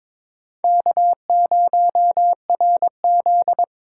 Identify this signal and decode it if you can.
Morse code, heard as a keyed tone, about 26 wpm, 700 Hz. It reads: K0RZ
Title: Speed Racing - USA 1x2 Call Signs 22wpm